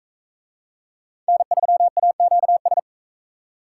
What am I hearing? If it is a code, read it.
Morse code, heard as a keyed tone, about 41 wpm, 700 Hz. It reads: N3AQS